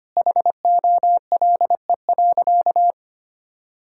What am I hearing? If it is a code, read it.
Morse code, heard as a keyed tone, about 25 wpm, 700 Hz. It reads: HOLE.
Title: Sentences from Top 1600 Words 25wpm